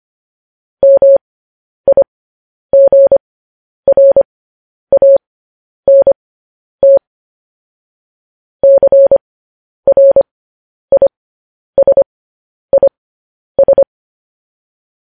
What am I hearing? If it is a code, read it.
Morse code, heard as a keyed tone, about 25 wpm, spaced as for 10 wpm, 550 Hz. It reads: MIGRANT CRISIS